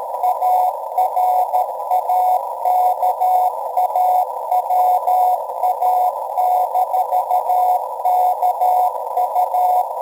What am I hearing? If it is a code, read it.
Morse code, heard as a keyed tone, about 13 wpm, 740 Hz, with noise. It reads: ARAKAWA-KU